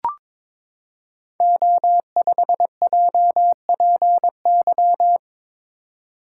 Morse code, heard as a keyed tone, about 22 wpm, 700 Hz. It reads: O5JPY
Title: Code Groups - 5 Letter:Numbers - 22wpm